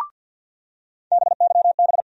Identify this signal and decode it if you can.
Morse code, heard as a keyed tone, about 52 wpm, 700 Hz. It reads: BYB
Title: USA Call Sign Postfixes - 3 Letters 50wpm